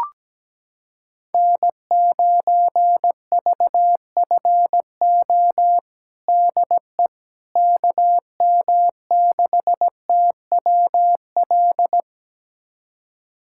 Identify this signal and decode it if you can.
Morse code, heard as a keyed tone, about 17 wpm, 700 Hz. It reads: N9VFO DE KM6TWL
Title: USA Callsign DE Callsign 17wpm